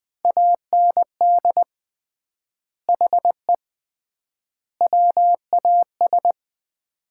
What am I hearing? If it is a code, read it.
Morse code, heard as a keyed tone, about 20 wpm, 700 Hz. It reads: AND HE WAS
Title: 3x Word-Spacing - Sets of 3 Words - 20wpm